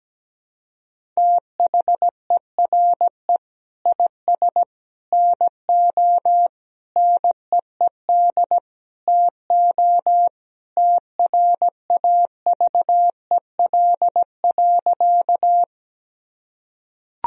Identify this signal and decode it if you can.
Morse code, heard as a keyed tone, about 17 wpm, 700 Hz. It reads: THERE IS NO NEED TO TRAVEL.
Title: Sentences from Top 1000 Words - Review Part 2 - 17wpm